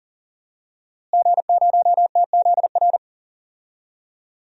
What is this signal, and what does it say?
Morse code, heard as a keyed tone, about 40 wpm, 700 Hz. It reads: G0TZR
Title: DXCC-100 Call Signs - 5 Characters - Repeated 2x - 40wpm